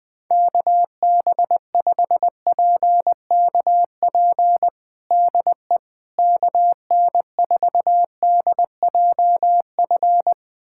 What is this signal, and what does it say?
Morse code, heard as a keyed tone, about 21 wpm, 700 Hz. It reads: KB5PKP DE KN4DJF
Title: USA Callsign DE Callsign - 6 Characters - 20wpm